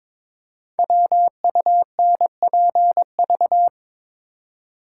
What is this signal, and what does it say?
Morse code, heard as a keyed tone, about 22 wpm, 700 Hz. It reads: WUNPV
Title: Code Groups - 5 Letters - 22wpm